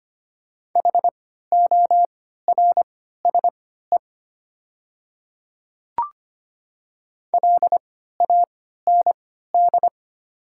Morse code, heard as a keyed tone, about 25 wpm, spaced as for 14 wpm, 700 Hz. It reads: HORSE LAND
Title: 3x Character Spacing - Top 300 Words - 25wpm